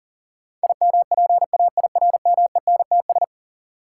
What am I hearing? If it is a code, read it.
Morse code, heard as a keyed tone, about 37 wpm, 700 Hz. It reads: IMPAIRMENTS
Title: Sets of 5 Words Encore - New Word Drill - 40wpm